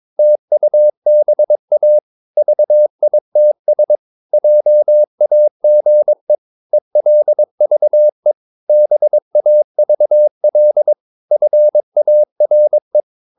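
Morse code, heard as a keyed tone, about 22 wpm, 600 Hz. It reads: TUBA VITS JAGE ELVE BAVL FARE
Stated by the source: Morse practice of short Danish words at 22 wpm, from CW med Gnister.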